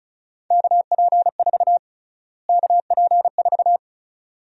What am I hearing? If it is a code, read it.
Morse code, heard as a keyed tone, about 34 wpm, 700 Hz. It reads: KP4 KP4
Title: USA Call Sign Prefixes - Repeated 2x - 35wpm